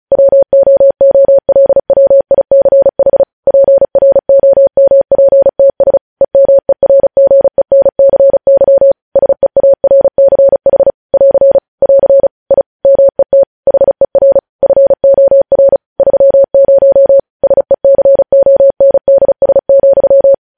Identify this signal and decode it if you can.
Morse code, heard as a keyed tone, about 36 wpm, 550 Hz. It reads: WOOLWICH PROMPTS EMERGENCY SEARCH <AR> <AR> I MET HER FOR 30 SECONDS,
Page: News Headlines in Morse Code at 35 WPM